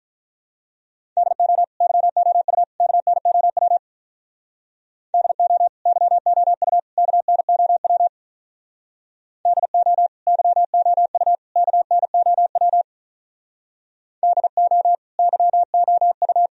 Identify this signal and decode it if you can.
Morse code, heard as a keyed tone, about 44 wpm, 700 Hz. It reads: DO YOU KNOW DO YOU KNOW DO YOU KNOW DO YOU